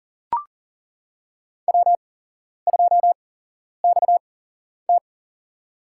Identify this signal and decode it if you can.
Morse code, heard as a keyed tone, about 40 wpm, spaced as for 11 wpm, 700 Hz. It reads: W2XT
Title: USA Call Signs - 4 Characters - 8x Character Spacing - 40wpm